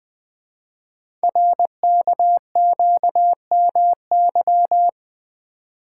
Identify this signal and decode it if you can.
Morse code, heard as a keyed tone, about 21 wpm, 700 Hz. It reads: RKQMY
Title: Code Groups - 5 Letter:Numbers - 20wpm